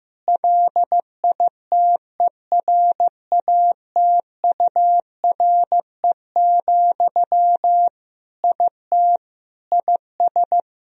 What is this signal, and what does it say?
Morse code, heard as a keyed tone, about 16 wpm, 700 Hz. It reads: LITERATURE, IT IS
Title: Sentences from Top 1000 Words - Review Part 3 - 15wpm